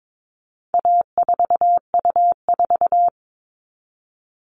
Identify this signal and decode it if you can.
Morse code, heard as a keyed tone, about 22 wpm, 700 Hz. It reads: A4U4
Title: All Cut Numbers - 4-Digit 22wpm